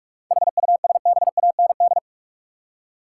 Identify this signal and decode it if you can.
Morse code, heard as a keyed tone, about 46 wpm, 700 Hz. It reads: HUSBAND
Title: Seven Letter Words - 45wpm